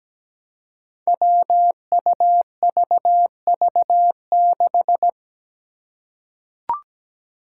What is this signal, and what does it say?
Morse code, heard as a keyed tone, about 17 wpm, 700 Hz. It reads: WUVV6
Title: Five Challenging Letter-Number Combinations 17wpm